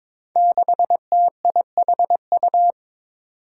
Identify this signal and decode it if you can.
Morse code, heard as a keyed tone, about 22 wpm, 700 Hz. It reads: 6TIHU
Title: Five Challenging Letter-Number Combinations 22wpm